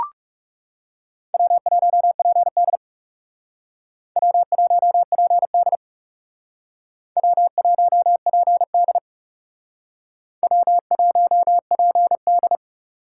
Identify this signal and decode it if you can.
Morse code, heard as a keyed tone, about 40 wpm, 700 Hz. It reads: W1PD W1PD W1PD W1PD